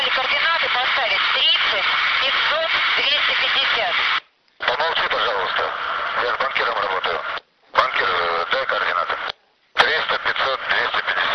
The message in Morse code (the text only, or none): none